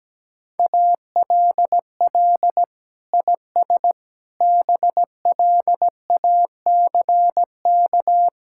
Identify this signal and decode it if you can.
Morse code, heard as a keyed tone, about 17 wpm, 700 Hz. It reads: ALL IS BLACK
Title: Sentences from Top 1600 Words 17wpm